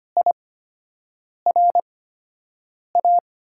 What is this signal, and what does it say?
Morse code, heard as a keyed tone, about 25 wpm, spaced as for 7 wpm, 700 Hz. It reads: IRA
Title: USA Call Signs - 5 Characters - 8x Character Spacing - 25wpm